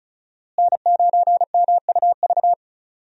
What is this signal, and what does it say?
Morse code, heard as a keyed tone, about 36 wpm, 700 Hz. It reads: N9MUV
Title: USA Call Signs - 5 Characters - Repeated 2x - 35wpm